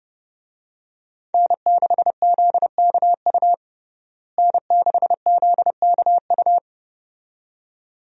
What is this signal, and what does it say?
Morse code, heard as a keyed tone, about 29 wpm, 700 Hz. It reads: N6ZKU N6ZKU
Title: USA Call Signs - 5 Characters - Repeated 2x - 30wpm